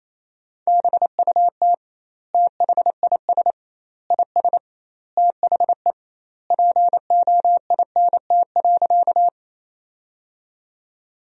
Morse code, heard as a keyed tone, about 27 wpm, 700 Hz. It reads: BUT THIS IS THE POINT.
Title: 2x Word-Spacing - Sentences from Top 200 Words - 28wpm